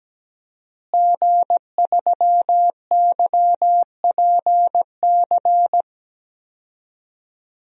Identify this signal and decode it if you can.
Morse code, heard as a keyed tone, about 17 wpm, 700 Hz. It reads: G3YPC
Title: DXCC-100 Call Signs - 5 Characters - Repeated 2x - 17wpm